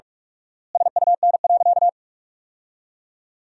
Morse code, heard as a keyed tone, about 41 wpm, 700 Hz. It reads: SUN.
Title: 4x Word-Spacing - Sentences from Top 300 Words - 45wpm